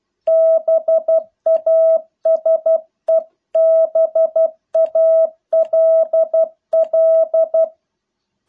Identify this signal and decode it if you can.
Morse code, heard as a keyed tone, about 12 wpm, 650 Hz. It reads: BASEBALL